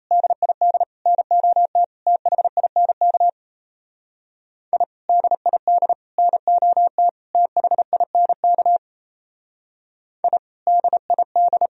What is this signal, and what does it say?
Morse code, heard as a keyed tone, about 34 wpm, 700 Hz. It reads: DID NOT THINK I DID NOT THINK I DID
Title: Speed-Racing - Sets of 4 Words 28wpm